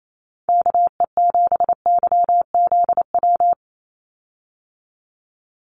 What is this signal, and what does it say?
Morse code, heard as a keyed tone, about 28 wpm, 700 Hz. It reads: KE7YZW
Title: USA Call Signs 28wpm